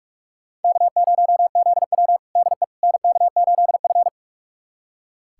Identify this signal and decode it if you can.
Morse code, heard as a keyed tone, about 45 wpm, 700 Hz. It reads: K0ZW DE NK8F
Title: USA Callsign DE Callsign - 4 Characters - 45wpm